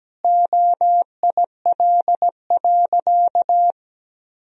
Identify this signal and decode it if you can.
Morse code, heard as a keyed tone, about 17 wpm, 700 Hz. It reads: OIL.
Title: Sentences from Top 1000 Words - Review Part 4 - 17wpm